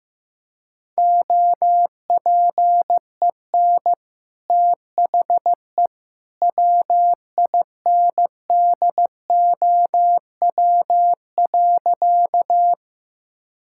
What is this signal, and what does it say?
Morse code, heard as a keyed tone, about 15 wpm, 700 Hz. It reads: OPEN THE WINDOW.
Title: Sentences from Top 1600 Words 15wpm